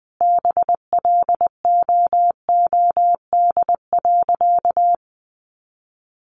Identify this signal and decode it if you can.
Morse code, heard as a keyed tone, about 20 wpm, 700 Hz. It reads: BLOOD.
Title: Sentences from Top 1100 Words 20wpm